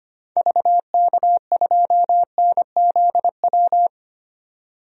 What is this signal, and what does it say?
Morse code, heard as a keyed tone, about 25 wpm, 700 Hz. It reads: VK2NZW
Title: DXCC-100 Call Signs - 6 Characters - 25wpm